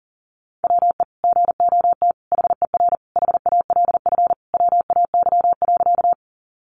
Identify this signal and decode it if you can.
Morse code, heard as a keyed tone, about 40 wpm, 700 Hz. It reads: WE GOT HER HALF WAY.
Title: Sentences from Top 400 Words 40wpm